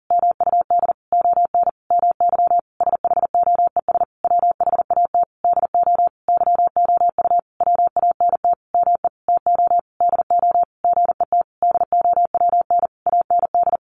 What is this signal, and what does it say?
Morse code, heard as a keyed tone, about 37 wpm, 700 Hz. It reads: MUD ON MY SHOES WHAT DO YOU WANT ME TO DO GET DOWN AND
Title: A Sound Of Thunder 40wpm